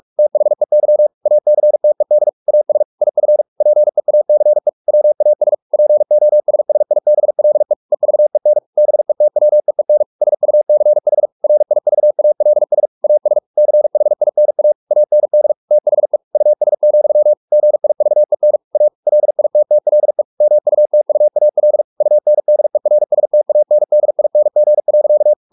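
Morse code, heard as a keyed tone, about 44 wpm, 600 Hz. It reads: THEY ACTED AS IF PEACE WAS POSSIBLE EVEN BETWEEN SUCH RIVALS AS CHINA AND THE US, GIVEN A LITTLE MUTUAL UNDERSTANDING.
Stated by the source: Morse practice at 45 wpm, the novel Never by Ken Follett in English, from CW med Gnister.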